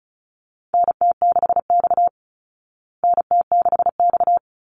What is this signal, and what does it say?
Morse code, heard as a keyed tone, about 34 wpm, 700 Hz. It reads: NT6X NT6X
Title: Call Signs Repeated 3x - 35wpm